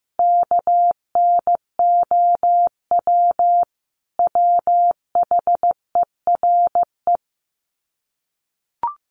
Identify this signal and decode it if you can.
Morse code, heard as a keyed tone, about 15 wpm, 700 Hz. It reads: KNOW WHERE
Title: Sets of 5 Words 15wpm